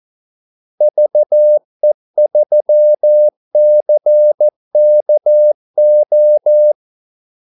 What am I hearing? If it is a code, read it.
Morse code, heard as a keyed tone, about 14 wpm, 600 Hz. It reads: VE3CKO
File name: Callsigns-14wpm